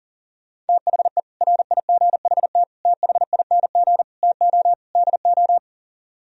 Morse code, heard as a keyed tone, about 50 wpm, 700 Hz. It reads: THE RIGHT THING TO DO
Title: Speed-Racing - Sets of 5 Words 40wpm